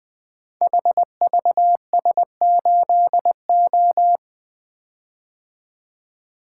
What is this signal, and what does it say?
Morse code, heard as a keyed tone, about 20 wpm, 700 Hz. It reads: HVS8O
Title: Code Groups - 5 Letter:Numbers - 20wpm